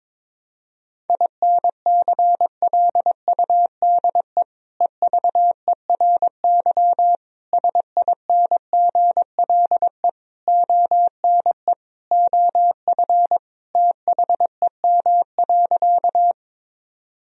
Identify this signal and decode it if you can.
Morse code, heard as a keyed tone, about 22 wpm, 700 Hz. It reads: INCLUDE EVERY SINGLE ONE OF THEM.
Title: Sentences from Top 1000 Words - Review Part 4 - 22wpm